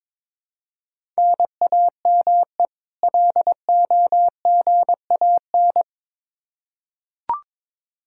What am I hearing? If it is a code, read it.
Morse code, heard as a keyed tone, about 22 wpm, 700 Hz. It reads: NAME LOGAN
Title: Practice QSO - Exchange 1 - Beginning - 22wpm